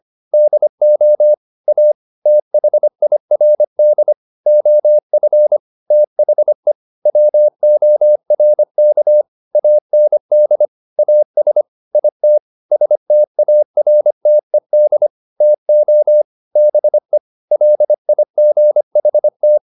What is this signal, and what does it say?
Morse code, heard as a keyed tone, about 25 wpm, 600 Hz. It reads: DO A THIRD OF THE WORK AND AS IT STARTED TO BE LIGHT
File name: the old man and the sea part 1 25wpm